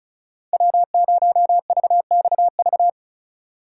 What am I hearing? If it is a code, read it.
Morse code, heard as a keyed tone, about 34 wpm, 700 Hz. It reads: W0VXV
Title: USA Call Signs - 5 Characters - 4x Character Spacing - 35wpm